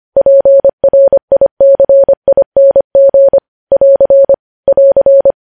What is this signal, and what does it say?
Morse code, heard as a keyed tone, about 25 wpm, 550 Hz. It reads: PRICING <AR> <AR>
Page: News Headlines in Morse Code at 25 WPM